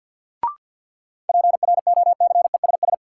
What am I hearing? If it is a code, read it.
Morse code, heard as a keyed tone, about 52 wpm, 700 Hz. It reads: PROCESS
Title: Sentences from Top 1400 Words - New Word Drill Alt - 50wpm